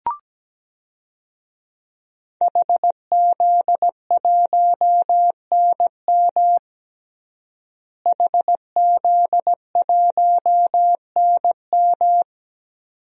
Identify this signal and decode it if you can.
Morse code, heard as a keyed tone, about 17 wpm, 700 Hz. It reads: HZ1NM HZ1NM
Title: DXCC-100 Call Signs - 5 Characters - Repeated 2x - 17wpm